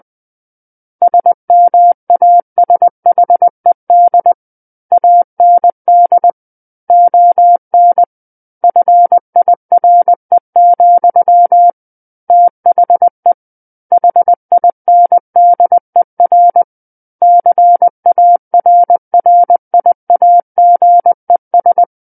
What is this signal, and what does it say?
Morse code, heard as a keyed tone, about 20 wpm, 700 Hz. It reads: SMASHED AND ON FIRE, THE HINDER CARRIAGES
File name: 20 WPM